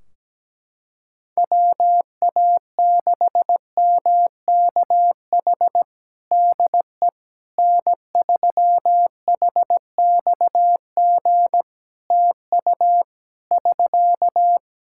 Morse code, heard as a keyed tone, about 17 wpm, 700 Hz. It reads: WA6MKH DE N3HXG TU <SK>
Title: Practice QSO - Exchange 4 - Ending - 17wpm